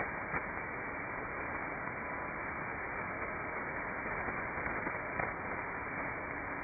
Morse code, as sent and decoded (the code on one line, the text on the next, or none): none